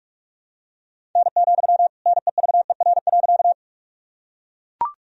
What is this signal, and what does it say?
Morse code, heard as a keyed tone, about 42 wpm, 700 Hz. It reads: N, NEVER.
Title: Sentences from Top 300 Words and Abbreviations 45wpm